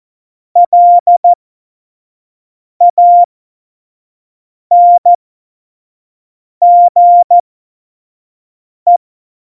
Morse code, heard as a keyed tone, about 14 wpm, spaced as for 5 wpm, 700 Hz. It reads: LANGE
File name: Morsekurs / Witze / Witz 16. April 2019, 25 F70